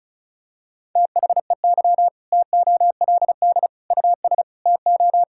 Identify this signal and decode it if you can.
Morse code, heard as a keyed tone, about 36 wpm, 700 Hz. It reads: THEY TOLD US TO
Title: Sentences from Top 1600 Words 35wpm